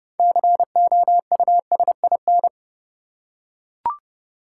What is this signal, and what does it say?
Morse code, heard as a keyed tone, about 32 wpm, 700 Hz. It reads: COUSIN
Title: Six Letter Words - 30wpm